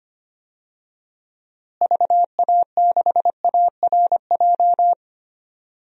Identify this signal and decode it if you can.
Morse code, heard as a keyed tone, about 25 wpm, 700 Hz. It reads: VA6ARJ
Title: DXCC-100 Call Signs - 6 Characters - Repeated 2x - 25wpm